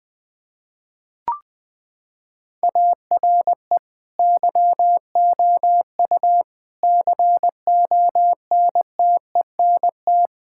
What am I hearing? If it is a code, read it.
Morse code, heard as a keyed tone, about 20 wpm, 700 Hz. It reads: ARE YOU CONTENT
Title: Sentences from Top 1000 Words - Review Part 6 - 20wpm